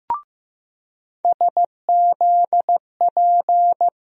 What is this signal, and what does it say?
Morse code, heard as a keyed tone, about 15 wpm, 700 Hz. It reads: SZP
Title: USA Call Sign Postfixes - 3 Letters 15wpm